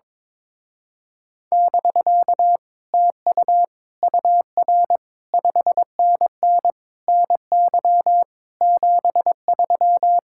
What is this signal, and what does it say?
Morse code, heard as a keyed tone, about 22 wpm, 700 Hz. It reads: <BK> TU UR 5NN NY 73
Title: Speed-Racing Common QSO Phrases Encore - 22wpm